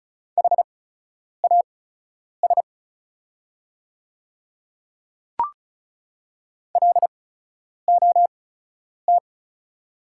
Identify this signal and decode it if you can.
Morse code, heard as a keyed tone, about 36 wpm, spaced as for 10 wpm, 690 Hz. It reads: HAS LOT
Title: 8x Character Spacing - Top 500 Words - 35wpm